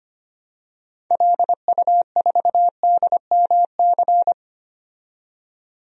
Morse code, heard as a keyed tone, about 25 wpm, 700 Hz. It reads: LU4DMC
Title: DXCC-100 Call Signs - 6 Characters - 25wpm